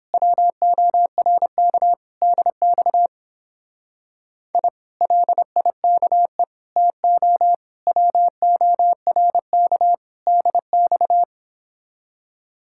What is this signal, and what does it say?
Morse code, heard as a keyed tone, about 29 wpm, 700 Hz. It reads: WORK DX I LIKE TO WORK DX